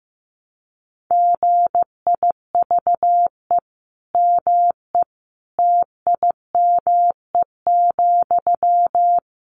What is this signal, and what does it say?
Morse code, heard as a keyed tone, about 15 wpm, 700 Hz. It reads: GIVE ME TIME,
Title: Sentences from Top 1300 Words 15wpm